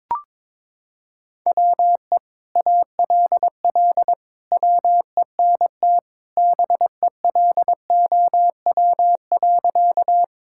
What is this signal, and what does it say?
Morse code, heard as a keyed tone, about 22 wpm, 700 Hz. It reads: WE ALL WENT BELOW.
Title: Sentences from Top 1000 Words - Review Part 5 - 22wpm